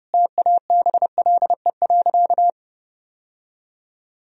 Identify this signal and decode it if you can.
Morse code, heard as a keyed tone, about 30 wpm, 700 Hz. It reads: TABLE.
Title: Sentences from Top 1000 Words - Review Part 2 - 30wpm